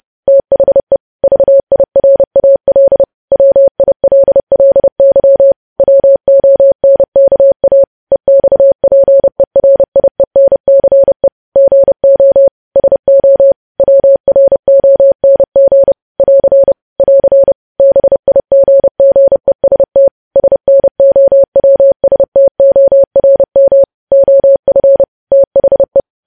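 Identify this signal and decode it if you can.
Morse code, heard as a keyed tone, about 30 wpm, 550 Hz. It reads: THE VIRAL WILLY WONKA EXPERIENCE GO SO WRONG <AR> <AR> BIGGEST SNOWSTORM OF THE
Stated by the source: Morse code practice of news headlines at 30 wpm, preceded by vvv.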